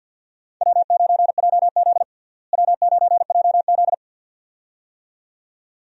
Morse code, heard as a keyed tone, about 51 wpm, 700 Hz. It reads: W9JZ W9JZ
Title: USA Call Signs - 4 Characters - Repeated 2x - 50wpm